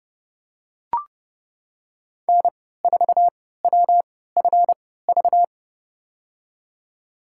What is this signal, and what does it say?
Morse code, heard as a keyed tone, about 32 wpm, spaced as for 17 wpm, 700 Hz. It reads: N4WFV
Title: USA Call Signs - 5 Characters - 3x Character Spacing - 30wpm